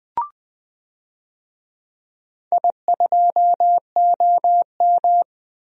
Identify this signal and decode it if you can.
Morse code, heard as a keyed tone, about 20 wpm, 700 Hz. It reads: I2OM
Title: DXCC-100 Call Signs - 4 Characters - 20wpm